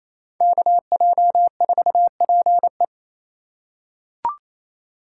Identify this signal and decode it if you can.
Morse code, heard as a keyed tone, about 28 wpm, 700 Hz. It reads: KJ4PE
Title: USA Call Signs - 5 Characters - 4x Character Spacing - 28wpm